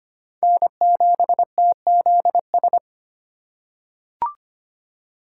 Morse code, heard as a keyed tone, about 25 wpm, 700 Hz. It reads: N7TZS
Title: USA Call Signs - 5 Characters - 3x Character Spacing - 25wpm